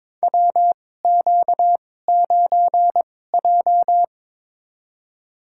Morse code, heard as a keyed tone, about 22 wpm, spaced as for 16 wpm, 700 Hz. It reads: WQ9J
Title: USA Call Signs - 4 Characters - 2x Character Spacing - 22wpm